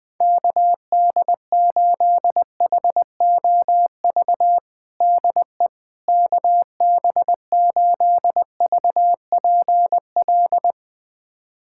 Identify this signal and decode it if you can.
Morse code, heard as a keyed tone, about 20 wpm, 700 Hz. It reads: KD8HOV DE KB8VPL
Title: USA Callsign DE Callsign - 6 Characters - 20wpm